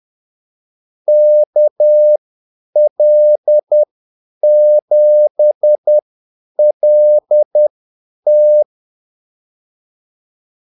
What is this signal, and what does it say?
Morse code, heard as a keyed tone, about 10 wpm, spaced as for 8 wpm, 600 Hz. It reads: KL7LT
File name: Callsigns-10-8wpm